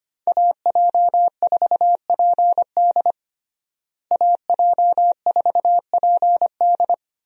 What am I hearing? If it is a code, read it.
Morse code, heard as a keyed tone, about 25 wpm, 700 Hz. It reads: AJ4PD AJ4PD
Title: USA Call Signs - 5 Characters - Repeated 2x - 25wpm